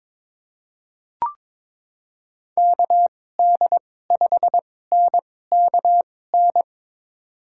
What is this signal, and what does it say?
Morse code, heard as a keyed tone, about 22 wpm, spaced as for 16 wpm, 700 Hz. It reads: KD5NKN
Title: USA Call Signs - 6 Characters - 2x Character Spacing - 22wpm